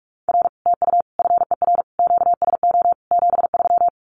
Morse code, heard as a keyed tone, about 44 wpm, 700 Hz. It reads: R TU FER QSO 73
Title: Common QSO Phrases Encore 45wpm